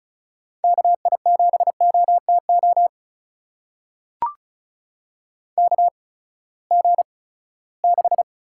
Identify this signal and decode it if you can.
Morse code, heard as a keyed tone, about 34 wpm, spaced as for 20 wpm, 700 Hz. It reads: KI7OTO KG6
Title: USA Call Signs - 6 Characters - 8x Character Spacing - 35wpm